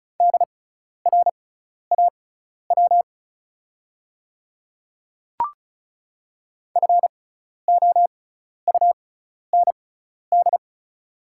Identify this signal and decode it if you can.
Morse code, heard as a keyed tone, about 36 wpm, spaced as for 12 wpm, 700 Hz. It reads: DRAW FOUND
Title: 6x Character Spacing - Top 300 Words - 35wpm